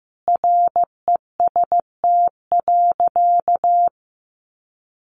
Morse code, heard as a keyed tone, about 15 wpm, 700 Hz. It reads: REST.
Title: Sentences from Top 500 Words 15wpm